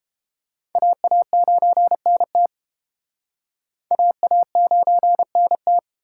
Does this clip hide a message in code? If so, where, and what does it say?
Yes: Morse code from 0.8 to 5.8 s, heard as a keyed tone, about 30 wpm, 700 Hz. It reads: AA9NT AA9NT